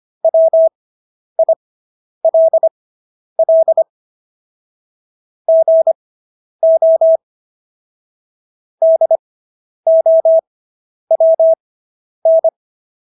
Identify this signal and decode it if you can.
Morse code, heard as a keyed tone, about 25 wpm, spaced as for 10 wpm, 650 Hz. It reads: WILL GO DOWN